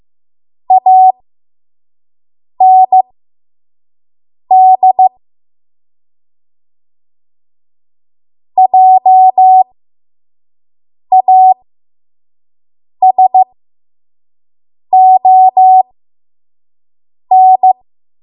Morse code, heard as a keyed tone, about 15 wpm, spaced as for 5 wpm, 750 Hz. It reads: AND JASON
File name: Morse / 05WPM